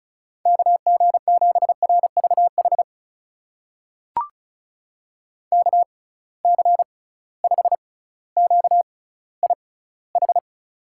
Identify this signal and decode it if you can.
Morse code, heard as a keyed tone, about 36 wpm, spaced as for 18 wpm, 700 Hz. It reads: KG7RVH KC5QIH
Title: USA Call Signs - 6 Characters - 6x Character Spacing - 35wpm